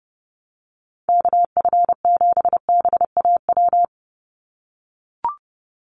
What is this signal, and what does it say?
Morse code, heard as a keyed tone, about 30 wpm, 700 Hz. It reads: KF7BAW
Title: USA Call Signs 30wpm